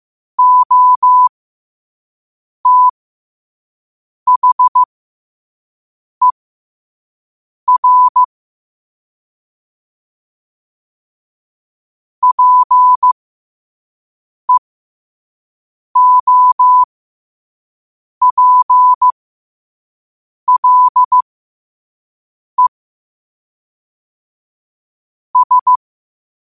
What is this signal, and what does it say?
Morse code, heard as a keyed tone, about 15 wpm, spaced as for 5 wpm, 1000 Hz. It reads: OTHER PEOPLE S
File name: qotdfiles / 5wpm